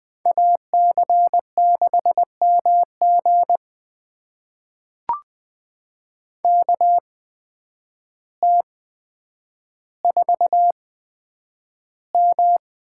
Morse code, heard as a keyed tone, about 20 wpm, spaced as for 9 wpm, 700 Hz. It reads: AC6MG KT4M
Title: USA Call Signs - 5 Characters - 8x Character Spacing - 20wpm